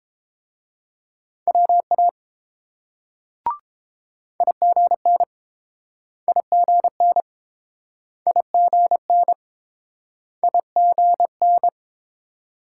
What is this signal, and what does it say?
Morse code, heard as a keyed tone, about 28 wpm, 700 Hz. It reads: WA IGN IGN IGN IGN